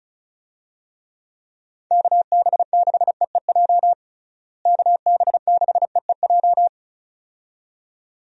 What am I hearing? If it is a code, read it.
Morse code, heard as a keyed tone, about 34 wpm, 700 Hz. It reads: KB6EEJ KB6EEJ
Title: USA Call Signs - 6 Characters - Repeated 2x - 35wpm